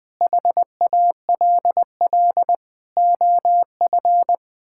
Morse code, heard as a keyed tone, about 21 wpm, 700 Hz. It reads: HALL OF
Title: Sentences with 6-Word Phrases - Spoken Beginning - 20wpm